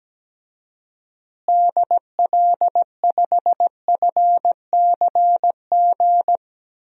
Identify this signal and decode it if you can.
Morse code, heard as a keyed tone, about 17 wpm, 700 Hz. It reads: DL5FCG
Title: DXCC-100 Call Signs - 6 Characters - Repeated 2x - 17wpm